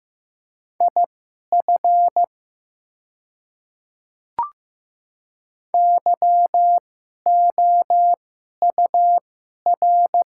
Morse code, heard as a keyed tone, about 15 wpm, spaced as for 11 wpm, 690 Hz. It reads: IF YOUR
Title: 2x Character Spacing - Top 300 Words - 15wpm